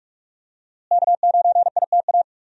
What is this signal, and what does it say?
Morse code, heard as a keyed tone, about 47 wpm, 700 Hz. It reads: K9ITA
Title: USA Call Signs - 5 Characters - Repeated 2x - 45wpm